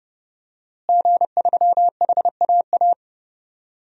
Morse code, heard as a keyed tone, about 30 wpm, 700 Hz. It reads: G3HAA
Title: DXCC-100 Call Signs - 5 Characters - Repeated 2x - 30wpm